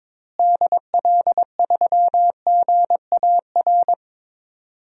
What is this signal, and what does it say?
Morse code, heard as a keyed tone, about 21 wpm, 700 Hz. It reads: DL3GAR
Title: DXCC-100 Call Signs - 6 Characters - Repeated 3x - 22wpm